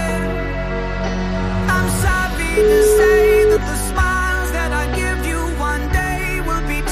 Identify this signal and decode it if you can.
Ringback tone (Europe, 425 Hz), with noise.